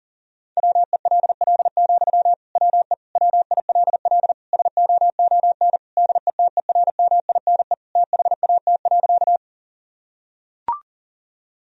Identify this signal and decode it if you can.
Morse code, heard as a keyed tone, about 37 wpm, 700 Hz. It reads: WELL, WE WILL SOON DETERMINE THAT.
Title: Sentences from Top 1000 Words - Review Part 4 - 40wpm